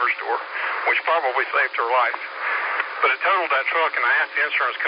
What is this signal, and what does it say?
Emergency Alert System (SAME) tones, heard as harsh buzzing bursts, with speech. It is heard from 0.5 to 4.5 s.